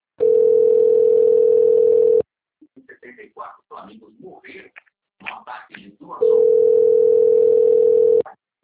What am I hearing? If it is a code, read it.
Ringback tone (North America), 440 + 480 Hz, with speech.
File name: rg-1600-unknown- - - .20066